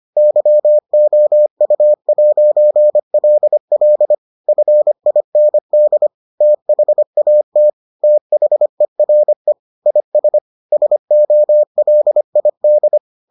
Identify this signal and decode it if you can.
Morse code, heard as a keyed tone, about 25 wpm, 600 Hz. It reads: YOU'LL FIND THAT THERE IS SOLID